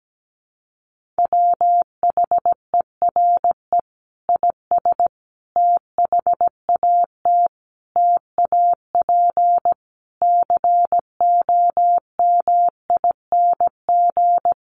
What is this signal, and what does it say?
Morse code, heard as a keyed tone, about 17 wpm, 700 Hz. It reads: WHERE IS THAT TAP COMING
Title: Sentences from Top 1400 Words 17wpm